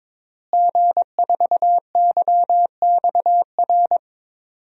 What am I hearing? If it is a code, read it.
Morse code, heard as a keyed tone, about 21 wpm, 700 Hz. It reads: G4YXR
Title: DXCC-100 Call Signs - 5 Characters - Repeated 2x - 22wpm